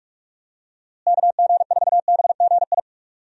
Morse code, heard as a keyed tone, about 42 wpm, 700 Hz. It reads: KG4BGI
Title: USA Call Signs - 6 Characters - Repeated 2x - 45wpm